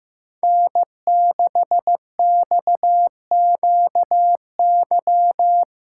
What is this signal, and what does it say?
Morse code, heard as a keyed tone, about 15 wpm, 700 Hz. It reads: N6XQY
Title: USA Callsign DE Callsign - 5 Characters - 15wpm